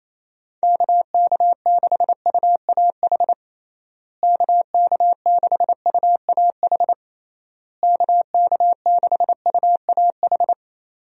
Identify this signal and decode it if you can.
Morse code, heard as a keyed tone, about 28 wpm, 700 Hz. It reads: KK6UAH KK6UAH KK6UAH
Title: USA Call Signs - 6 Characters - Repeated 3x - 28wpm